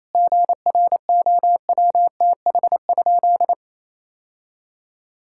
Morse code, heard as a keyed tone, about 28 wpm, 700 Hz. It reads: GROWTH?
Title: Sentences from Top 1000 Words - Review Part 5 - 28wpm